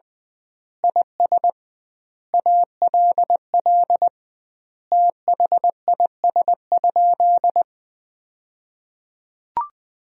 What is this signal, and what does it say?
Morse code, heard as a keyed tone, about 20 wpm, 700 Hz. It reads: IS ALL THIS?
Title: 2x Word-Spacing - Sentences from Top 100 Words - 20wpm